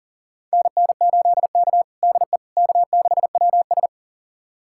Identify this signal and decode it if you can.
Morse code, heard as a keyed tone, about 40 wpm, 700 Hz. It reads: NN8K DE K6WS
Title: USA Callsign DE Callsign - 4 Characters - 40wpm